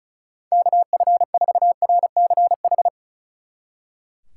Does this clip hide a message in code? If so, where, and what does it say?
Yes: Morse code from 0.5 to 2.9 s, heard as a keyed tone, about 47 wpm, 700 Hz. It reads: KF4RCH